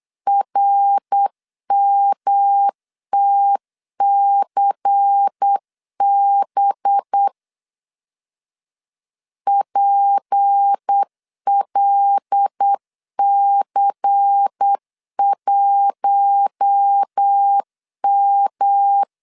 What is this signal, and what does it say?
Morse code, heard as a keyed tone, about 8 wpm, 800 Hz. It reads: RMTCB PLC1M
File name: LEZIONI / 6 PKX0 / 6 P 8wpm 3025